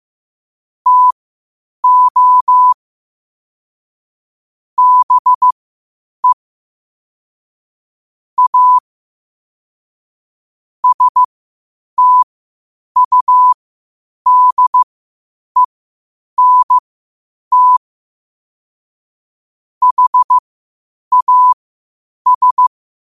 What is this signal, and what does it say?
Morse code, heard as a keyed tone, about 15 wpm, spaced as for 8 wpm, 1000 Hz. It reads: TO BE A STUDENT HAS
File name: qotdfiles / 8wpm